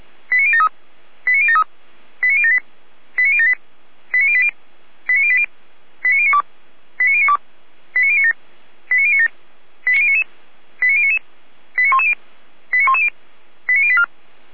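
Five-tone selective call (ZVEI), heard as quick runs of five tones, tones of about 70 ms. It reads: 89073 89073 89078 89078 89080 89080 89092 89092 89097 89097 89099 89099 89110 89110 89084
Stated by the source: ZVEI five-tone call received by SDR, fire brigade Feuerwehr Orlamünde.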